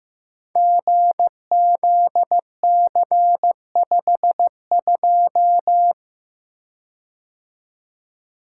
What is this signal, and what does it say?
Morse code, heard as a keyed tone, about 15 wpm, 700 Hz. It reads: GZC52
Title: Code Groups - 5 Letter:Numbers - 15wpm